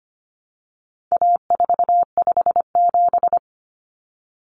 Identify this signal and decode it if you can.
Morse code, heard as a keyed tone, about 25 wpm, 700 Hz. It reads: A457
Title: Common Cut Numbers - 4-Digit 25wpm